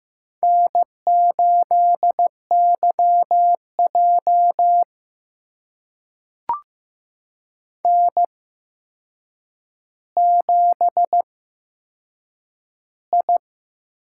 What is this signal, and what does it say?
Morse code, heard as a keyed tone, about 15 wpm, spaced as for 7 wpm, 700 Hz. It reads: N8YJ N7I
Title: USA Call Signs - 4 Characters - 8x Character Spacing - 15wpm